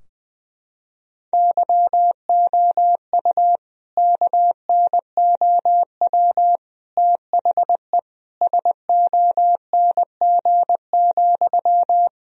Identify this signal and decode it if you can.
Morse code, heard as a keyed tone, about 20 wpm, 700 Hz. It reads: YOU KNOW THE SONG,
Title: Sentences from Top 1000 Words - Review Part 2 - 20wpm